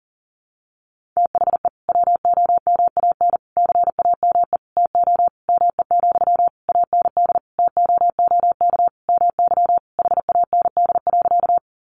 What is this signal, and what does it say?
Morse code, heard as a keyed tone, about 40 wpm, 700 Hz. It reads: THE WOMAN CAME TO ME, AND TOOK MY HAND.
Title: Sentences from Top 1100 Words 40wpm